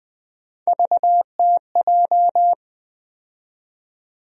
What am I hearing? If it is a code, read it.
Morse code, heard as a keyed tone, about 20 wpm, 700 Hz. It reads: VTJ
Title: USA Call Sign Postfixes - 3 Letters 20wpm